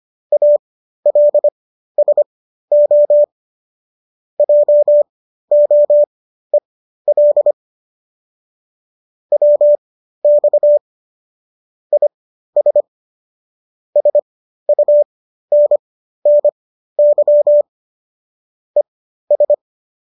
Morse code, heard as a keyed tone, about 25 wpm, spaced as for 13 wpm, 590 Hz. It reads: ALSO JOEL WX IS SUNNY ES
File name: qso214 13